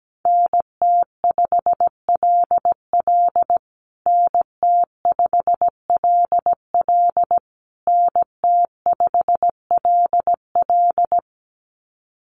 Morse code, heard as a keyed tone, about 17 wpm, 700 Hz. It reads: NT5LL NT5LL NT5LL
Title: USA Call Signs Repeated 3x 17wpm